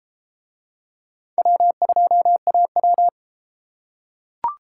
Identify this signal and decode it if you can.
Morse code, heard as a keyed tone, about 30 wpm, 700 Hz. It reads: W2AW